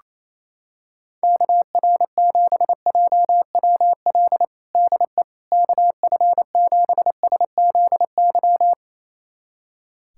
Morse code, heard as a keyed tone, about 28 wpm, 700 Hz. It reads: KR7JWL DE KF7SZY
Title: USA Callsign DE Callsign - 6 Characters - 28wpm